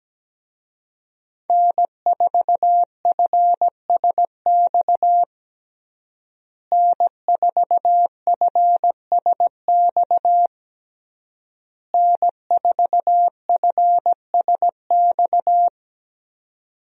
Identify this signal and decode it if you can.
Morse code, heard as a keyed tone, about 17 wpm, 700 Hz. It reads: N4FSX N4FSX N4FSX
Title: USA Call Signs - 5 Characters - Repeated 3x - 17wpm